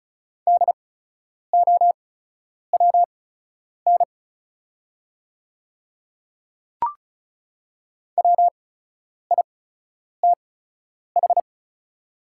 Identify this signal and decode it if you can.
Morse code, heard as a keyed tone, about 36 wpm, spaced as for 10 wpm, 700 Hz. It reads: DOWN WITH
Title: 8x Character Spacing - Top 300 Words - 35wpm